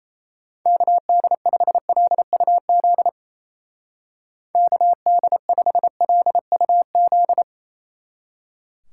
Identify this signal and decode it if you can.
Morse code, heard as a keyed tone, about 37 wpm, 700 Hz. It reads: KD5LUZ KD5LUZ